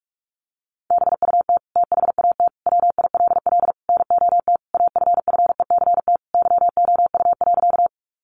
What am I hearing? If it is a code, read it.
Morse code, heard as a keyed tone, about 44 wpm, 700 Hz. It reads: BUT THAT WILL NOT AFFECT YOU.
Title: Sentences from Top 900 Words 45wpm